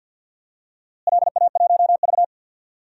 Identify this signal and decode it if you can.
Morse code, heard as a keyed tone, about 51 wpm, 700 Hz. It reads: LA1V
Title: DXCC-100 Call Signs - 4 Characters - Repeated 2x - 50wpm